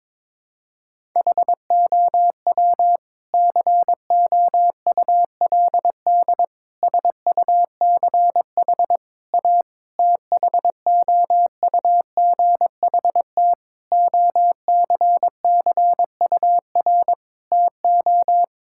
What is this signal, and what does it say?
Morse code, heard as a keyed tone, about 22 wpm, 700 Hz. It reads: HOW COULD SUCH A THOUGHT OCCUR TO
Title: Sentences from Top 1000 Words - Review Part 6 - 22wpm